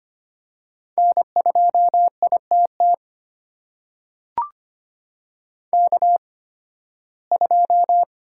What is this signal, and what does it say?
Morse code, heard as a keyed tone, about 25 wpm, spaced as for 16 wpm, 700 Hz. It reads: N2ITT K2
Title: USA Call Signs - 8x Character Spacing - 25wpm